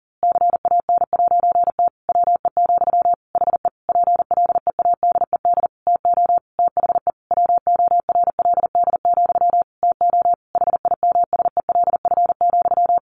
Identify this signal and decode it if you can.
Morse code, heard as a keyed tone, about 37 wpm, 700 Hz. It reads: CAN'T WE, HE PLEADED TO THE WORLD, TO HIMSELF,
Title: A Sound Of Thunder 40wpm